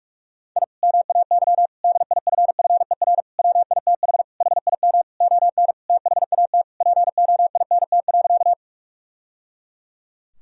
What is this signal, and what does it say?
Morse code, heard as a keyed tone, about 41 wpm, 700 Hz. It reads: I MAY DIFFER WITH HIM ON THAT POINT.
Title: Sentences from Top 1000 Words - Review Part 1 - 45wpm